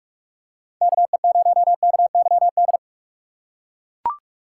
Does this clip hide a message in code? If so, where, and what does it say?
Yes: Morse code from 0.8 to 2.8 s, heard as a keyed tone, about 41 wpm, 700 Hz. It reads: KE0KYD